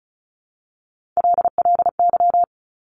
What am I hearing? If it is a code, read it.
Morse code, heard as a keyed tone, about 36 wpm, 700 Hz. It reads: LLY
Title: Top Three Letter Combinations in Words 35wpm